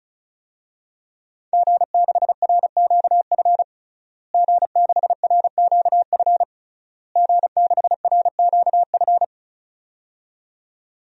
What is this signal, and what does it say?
Morse code, heard as a keyed tone, about 35 wpm, 700 Hz. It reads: G6RQF G6RQF G6RQF
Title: DXCC-100 Call Signs - 5 Characters - Repeated 3x - 35wpm